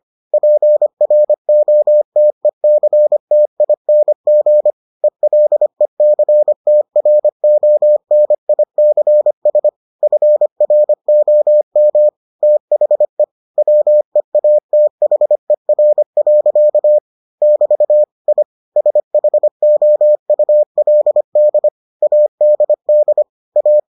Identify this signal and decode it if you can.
Morse code, heard as a keyed tone, about 25 wpm, 600 Hz. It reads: PROTECTING ELECTRONICS FROM THE WEATHER. <BT> I SHOULD ADD A